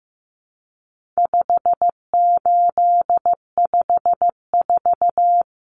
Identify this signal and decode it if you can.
Morse code, heard as a keyed tone, about 15 wpm, 700 Hz. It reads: 5854
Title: Common Cut Numbers - 4-Digit 15wpm